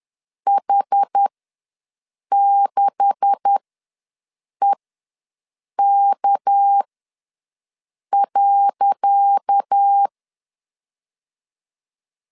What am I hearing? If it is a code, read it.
Morse code, heard as a keyed tone, about 10 wpm, spaced as for 6 wpm, 800 Hz. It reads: H6EK.
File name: LEZIONI / 10 SIGNS / 10C DOT ws 10wpm 4213